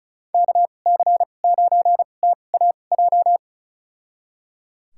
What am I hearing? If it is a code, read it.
Morse code, heard as a keyed tone, about 34 wpm, spaced as for 25 wpm, 700 Hz. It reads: KC9TAJ
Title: USA Call Signs - 6 Characters - 2x Character Spacing - 35wpm